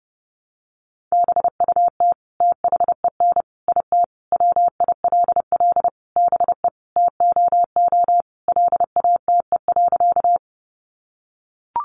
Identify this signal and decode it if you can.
Morse code, heard as a keyed tone, about 30 wpm, 700 Hz. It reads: BUT THEN IT WILL BE TOO LATE.
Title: Sentences from Top 300 Words 30wpm